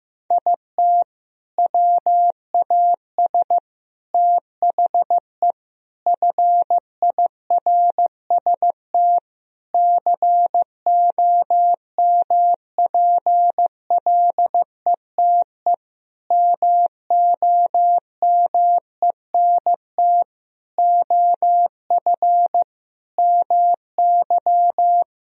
Morse code, heard as a keyed tone, about 15 wpm, 700 Hz. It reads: IT WAS THE FIRST COMPLETE MOMENT OF MY
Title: Sentences from Top 1000 Words - Review Part 2 - 15wpm